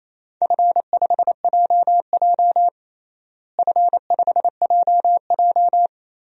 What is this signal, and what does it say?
Morse code, heard as a keyed tone, about 28 wpm, 700 Hz. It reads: F5JJ F5JJ
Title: DXCC-100 Call Signs - 4 Characters - Repeated 3x - 28wpm